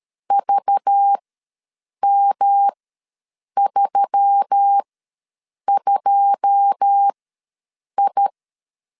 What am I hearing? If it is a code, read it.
Morse code, heard as a keyed tone, about 13 wpm, spaced as for 7 wpm, 800 Hz. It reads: VM32I
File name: LEZIONI / 9 28 ws 12wpm 3354